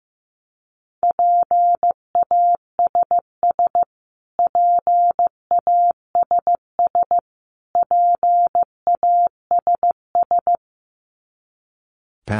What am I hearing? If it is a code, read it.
Morse code, heard as a keyed tone, about 15 wpm, 700 Hz. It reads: PASS PASS PASS
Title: Top 500 Words Repeated 15wpm